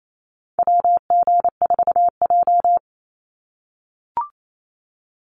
Morse code, heard as a keyed tone, about 28 wpm, 700 Hz. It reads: WG4J
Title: USA Call Signs 28wpm